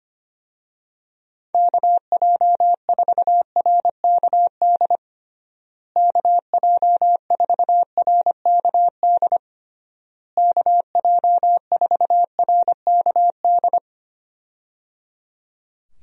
Morse code, heard as a keyed tone, about 25 wpm, 700 Hz. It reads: KJ4RKD KJ4RKD KJ4RKD
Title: USA Call Signs - 6 Characters - Repeated 3x - 25wpm